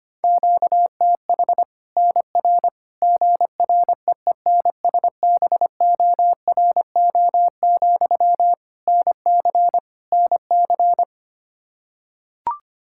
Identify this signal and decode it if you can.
Morse code, heard as a keyed tone, about 25 wpm, 700 Hz. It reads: QTH NR GREENSBORO, NC NC
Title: Practice QSO - Exchange 1 - Beginning - 25wpm